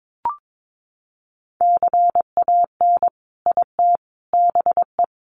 Morse code, heard as a keyed tone, about 22 wpm, 700 Hz. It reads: CAN IT BE
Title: Sentences from Top 400 Words 22wpm